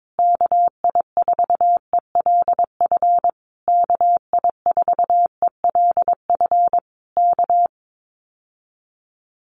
Morse code, heard as a keyed tone, about 22 wpm, 700 Hz. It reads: KI4ELF KI4ELF K
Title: Common QSO Phrases Encore 22wpm